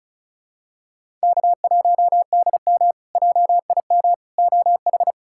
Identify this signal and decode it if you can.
Morse code, heard as a keyed tone, about 36 wpm, 700 Hz. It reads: K1DM JIM OH
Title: K1USN SST QSO Practice - 35wpm